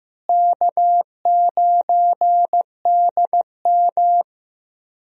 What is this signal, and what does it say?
Morse code, heard as a keyed tone, about 15 wpm, 700 Hz. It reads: K9DM
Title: USA Call Signs - 4 Characters - Repeated 3x - 15wpm